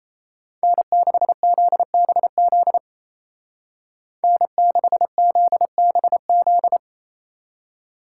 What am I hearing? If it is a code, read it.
Morse code, heard as a keyed tone, about 30 wpm, 700 Hz. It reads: N6ZBZ N6ZBZ
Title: Speed Racing - USA 1x3 Call Signs 28wpm